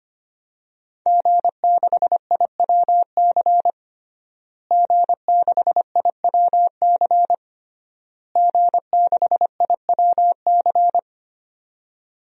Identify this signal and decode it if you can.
Morse code, heard as a keyed tone, about 25 wpm, 700 Hz. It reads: G6IWC G6IWC G6IWC
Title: DXCC-100 Call Signs - 5 Characters - Repeated 3x - 25wpm